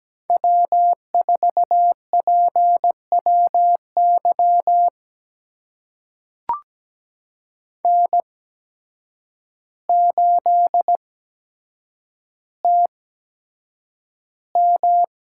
Morse code, heard as a keyed tone, about 17 wpm, spaced as for 8 wpm, 700 Hz. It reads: W4PWY N8TM
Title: USA Call Signs - 8x Character Spacing - 17wpm